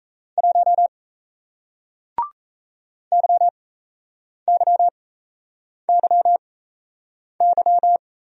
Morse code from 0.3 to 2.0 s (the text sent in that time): 1